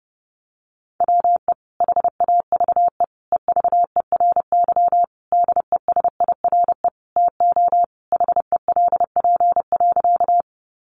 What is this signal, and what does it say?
Morse code, heard as a keyed tone, about 30 wpm, 700 Hz. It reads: WE HAVE EVERY DESIRE TO HELP.
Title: Sentences from Top 1100 Words 30wpm